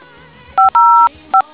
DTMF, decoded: 505